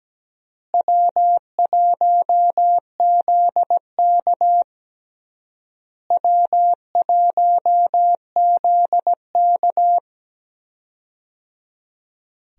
Morse code, heard as a keyed tone, about 17 wpm, 700 Hz. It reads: W1ZK W1ZK
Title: USA Call Signs - 4 Characters - Repeated 3x - 17wpm